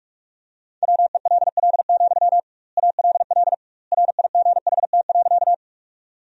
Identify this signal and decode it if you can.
Morse code, heard as a keyed tone, about 46 wpm, 700 Hz. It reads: WELL, ALL RIGHT.
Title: 2x Word-Spacing - Sentences from Top 200 Words - 45wpm